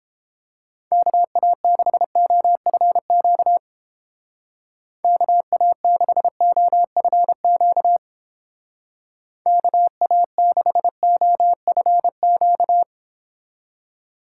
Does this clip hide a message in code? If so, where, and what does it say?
Yes: Morse code from 0.9 to 12.8 s, heard as a keyed tone, about 31 wpm, 700 Hz. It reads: KA6OFQ KA6OFQ KA6OFQ